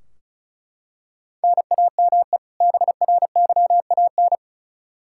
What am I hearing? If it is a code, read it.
Morse code, heard as a keyed tone, about 36 wpm, 700 Hz. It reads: NAME BRYAN
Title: K1USN SST QSO Practice 35wpm